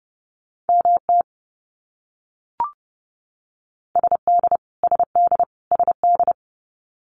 Morse code, heard as a keyed tone, about 30 wpm, 700 Hz. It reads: MT SD SD SD
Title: US States Repeated 3x 30wpm